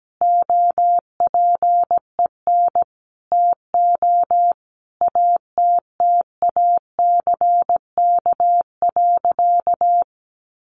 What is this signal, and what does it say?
Morse code, heard as a keyed tone, about 17 wpm, 700 Hz. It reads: OPEN TO ATTACK.
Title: Sentences from Top 1500 Words 17wpm